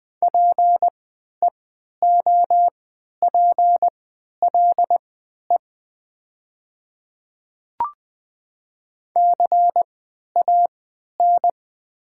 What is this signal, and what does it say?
Morse code, heard as a keyed tone, about 20 wpm, spaced as for 11 wpm, 700 Hz. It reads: PEOPLE CAN
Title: 3x Character Spacing - Top 100 Words - 20wpm